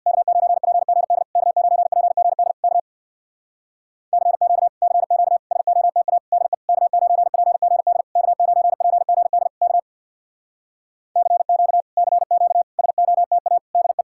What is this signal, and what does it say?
Morse code, heard as a keyed tone, about 65 wpm, 700 Hz. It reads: K9PGD K9PGD K CQ CQ SOTA DE K9PGD K9PGD K CQ CQ SOTA DE